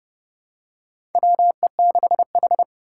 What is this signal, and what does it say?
Morse code, heard as a keyed tone, about 31 wpm, 700 Hz. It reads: WE6H
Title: USA Call Signs - 4 Characters - 4x Character Spacing - 30wpm